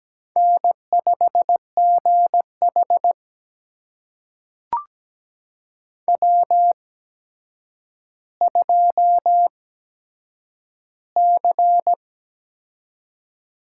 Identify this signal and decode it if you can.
Morse code, heard as a keyed tone, about 17 wpm, spaced as for 8 wpm, 700 Hz. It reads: N5GH W2C
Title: USA Call Signs - 4 Characters - 8x Character Spacing - 17wpm